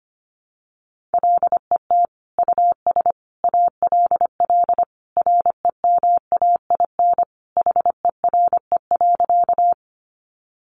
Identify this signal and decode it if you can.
Morse code, heard as a keyed tone, about 25 wpm, 700 Hz. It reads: LET US ALL REMAIN HERE.
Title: Sentences from Top 1200 Words 25wpm